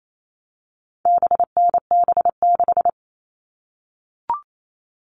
Morse code, heard as a keyed tone, about 28 wpm, 700 Hz. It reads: BNB6
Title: All Cut Numbers - 4-Digit 28wpm